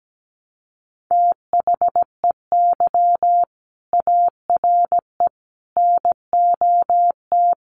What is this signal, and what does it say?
Morse code, heard as a keyed tone, about 17 wpm, 700 Hz. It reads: THEY ARE NOT
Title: Sets of 5 Words Encore2 17wpm